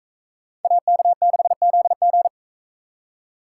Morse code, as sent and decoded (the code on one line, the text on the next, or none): .- -.- -.... --.. --.
AK6ZG